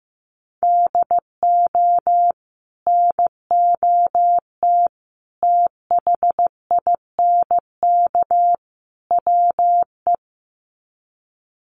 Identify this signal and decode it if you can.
Morse code, heard as a keyed tone, about 15 wpm, 700 Hz. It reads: DO NOT THINK WE
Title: Sets of 4 Words 15wpm